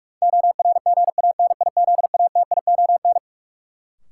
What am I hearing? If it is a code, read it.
Morse code, heard as a keyed tone, about 44 wpm, 700 Hz. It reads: ORGANIZATION
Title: Sentences from Top 1000 Words - Review Part 5 - New Word Drill - 45wpm